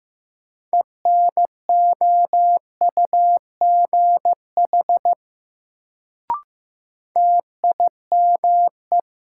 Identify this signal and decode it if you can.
Morse code, heard as a keyed tone, about 15 wpm, 700 Hz. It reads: ENOUGH TIME
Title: Sets of 2 Words - New Word Drill - 15wpm